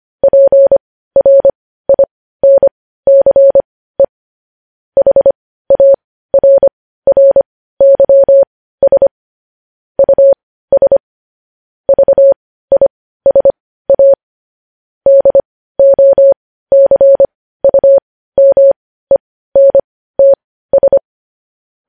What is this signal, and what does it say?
Morse code, heard as a keyed tone, about 25 wpm, spaced as for 15 wpm, 550 Hz. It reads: PRINCE HARRYS US VISA DOCUMENTS
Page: news headlines in morse code at 15 wpm